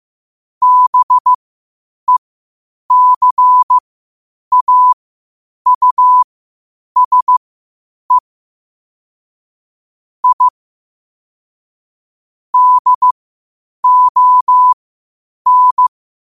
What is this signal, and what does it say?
Morse code, heard as a keyed tone, about 15 wpm, spaced as for 8 wpm, 1000 Hz. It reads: BECAUSE I DON
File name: qotdfiles / 8wpm